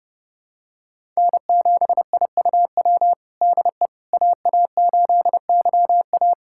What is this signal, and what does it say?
Morse code, heard as a keyed tone, about 30 wpm, 700 Hz. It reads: N7IUW DE AA8YA
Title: USA Callsign DE Callsign - 5 Characters - 30wpm